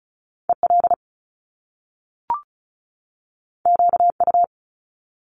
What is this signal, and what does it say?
Morse code, heard as a keyed tone, about 36 wpm, 700 Hz. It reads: EL QU
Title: Top Two Letter Combinations in Words 35wpm